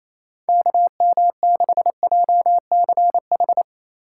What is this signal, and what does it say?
Morse code, heard as a keyed tone, about 28 wpm, 700 Hz. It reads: KM6JCH
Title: USA Call Signs - 2x Character Spacing - 28wpm